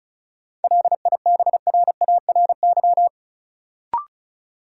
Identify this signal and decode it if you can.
Morse code, heard as a keyed tone, about 36 wpm, 700 Hz. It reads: LIBRARY
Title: Sentences from Top 600 Words - New Word Drill - 35wpm